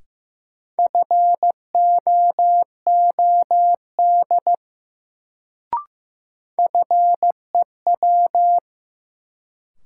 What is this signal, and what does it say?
Morse code, heard as a keyed tone, about 15 wpm, 700 Hz. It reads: FOOD FEW
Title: Sentences from Top 300 Words - New Word Drill - 15wpm